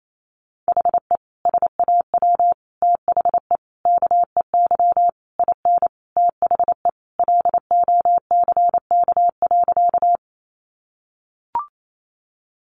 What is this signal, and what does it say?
Morse code, heard as a keyed tone, about 28 wpm, 700 Hz. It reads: HE SAW THE KEY IN THE LOCK.
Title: Sentences from Top 1300 Words 28wpm